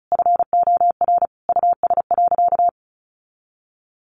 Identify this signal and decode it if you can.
Morse code, heard as a keyed tone, about 34 wpm, 700 Hz. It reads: FOR US.
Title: Sentences from Top 1400 Words 35wpm